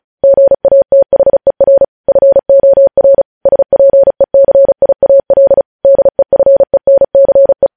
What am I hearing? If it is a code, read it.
Morse code, heard as a keyed tone, about 36 wpm, 550 Hz. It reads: GATHER FOR SPECIAL DEFENCE
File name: podcast / News headlines-1x-35x35